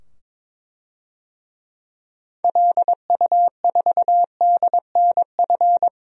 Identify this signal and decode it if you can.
Morse code, heard as a keyed tone, about 22 wpm, 700 Hz. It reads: LU4DNF
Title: DXCC-100 Call Signs - 6 Characters - 22wpm